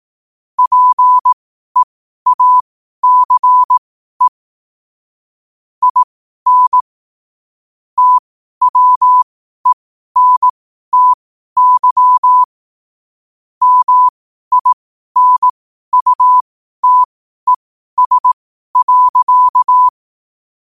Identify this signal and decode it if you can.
Morse code, heard as a keyed tone, about 18 wpm, spaced as for 12 wpm, 1000 Hz. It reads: PEACE IN TWENTY MINUTES.
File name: qotdfiles / 12wpm